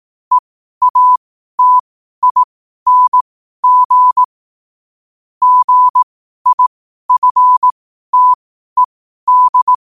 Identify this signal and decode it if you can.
Morse code, heard as a keyed tone, about 18 wpm, spaced as for 12 wpm, 1000 Hz. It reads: EATING GIFTED
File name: qotdfiles / 12wpm